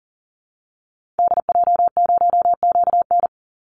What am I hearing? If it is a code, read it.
Morse code, heard as a keyed tone, about 40 wpm, 700 Hz. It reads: DJ0QN
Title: Call Signs Repeated 3x - 40wpm